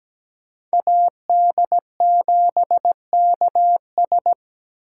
Morse code, heard as a keyed tone, about 20 wpm, 700 Hz. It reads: AD7KS
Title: Speed Racing - USA 2x2 Call Signs 17wpm